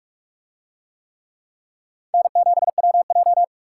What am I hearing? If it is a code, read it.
Morse code, heard as a keyed tone, about 44 wpm, 700 Hz. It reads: N7WJ